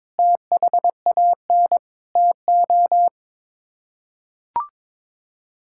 Morse code, heard as a keyed tone, about 22 wpm, 700 Hz. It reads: THAN TO
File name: practice / 22wpm / Sets of 7 Words - 22wpm